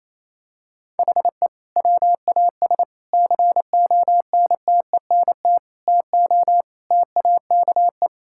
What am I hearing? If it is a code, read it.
Morse code, heard as a keyed tone, about 28 wpm, 700 Hz. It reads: HE WAS CONTENT TO TAKE
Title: Sentences from Top 1000 Words - Review Part 6 - 28wpm